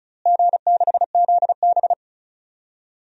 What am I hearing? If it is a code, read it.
Morse code, heard as a keyed tone, about 36 wpm, 700 Hz. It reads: G6ZB
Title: DXCC-100 Call Signs - 4 Characters - Repeated 3x - 35wpm